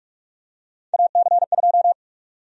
Morse code, heard as a keyed tone, about 42 wpm, 700 Hz. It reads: AC2
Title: USA Call Sign Prefixes - 45wpm